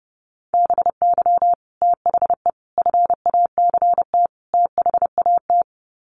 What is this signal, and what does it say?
Morse code, heard as a keyed tone, about 30 wpm, 700 Hz. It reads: BY THE FACT THAT
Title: Sets of 4 Words 30wpm